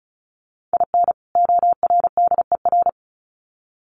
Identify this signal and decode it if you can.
Morse code, heard as a keyed tone, about 34 wpm, 700 Hz. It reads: IN ORDER
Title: Sets of 2 Words 35wpm